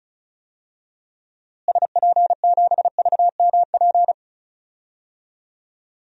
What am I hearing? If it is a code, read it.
Morse code, heard as a keyed tone, about 36 wpm, 700 Hz. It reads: SP7VMP
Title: DXCC-100 Call Signs - 6 Characters - Repeated 3x - 35wpm